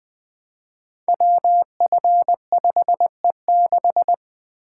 Morse code, heard as a keyed tone, about 20 wpm, 700 Hz. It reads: WF5E6
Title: Code Groups - 5 Letter:Numbers - 20wpm